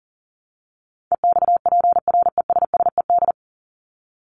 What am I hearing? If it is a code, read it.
Morse code, heard as a keyed tone, about 40 wpm, 700 Hz. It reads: EXPRESSED
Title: ED Suffix - Top 100 Words with Base Words 40wpm